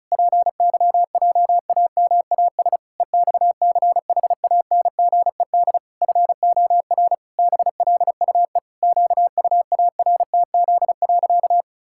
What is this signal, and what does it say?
Morse code, heard as a keyed tone, about 36 wpm, 700 Hz. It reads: PYJAMAS EXCHANGED FOR BLUE QUARTZ.
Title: Pangram Challenge v2 - 35wpm